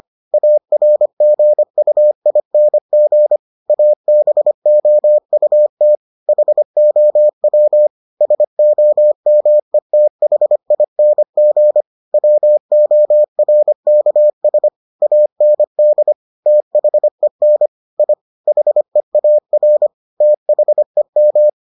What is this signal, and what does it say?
Morse code, heard as a keyed tone, about 25 wpm, 600 Hz. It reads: ARGUING ABOUT HOW SOMETHING WORKS AND THEN I HEAR THEM